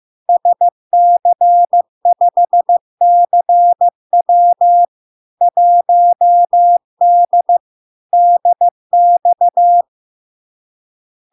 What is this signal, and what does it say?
Morse code, heard as a keyed tone, about 15 wpm, 700 Hz. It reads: SC5CW 1D DX
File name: LARK-FD-2022-Log-in-CW-15wpm-700-Hz